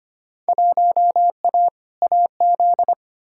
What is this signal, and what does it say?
Morse code, heard as a keyed tone, about 25 wpm, 700 Hz. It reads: 1A AZ
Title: Field Day Exchanges - 25wpm